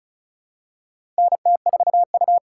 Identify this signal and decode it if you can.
Morse code, heard as a keyed tone, about 34 wpm, 700 Hz. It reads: NT4U
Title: USA Call Signs - 4 Characters - 4x Character Spacing - 35wpm